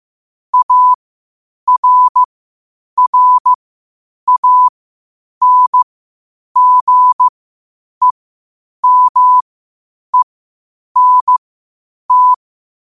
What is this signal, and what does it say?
Morse code, heard as a keyed tone, about 15 wpm, spaced as for 8 wpm, 1000 Hz. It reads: ARRANGEMENT
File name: qotdfiles / 8wpm